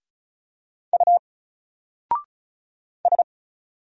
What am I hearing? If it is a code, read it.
Morse code, heard as a keyed tone, about 34 wpm, 700 Hz. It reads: U S
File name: Single-Letter-Number-35wpm